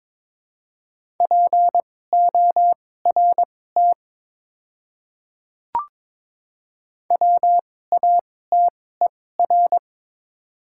Morse code, heard as a keyed tone, about 21 wpm, spaced as for 16 wpm, 700 Hz. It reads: PORT WATER
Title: 2x Character Spacing - Top 200 Words - 22wpm